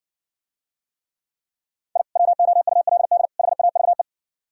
Morse code, heard as a keyed tone, about 65 wpm, 700 Hz. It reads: I WOULD HAVE